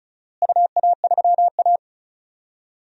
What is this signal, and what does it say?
Morse code, heard as a keyed tone, about 36 wpm, 700 Hz. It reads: UA3A
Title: DXCC-100 Call Signs - 4 Characters - Repeated 2x - 35wpm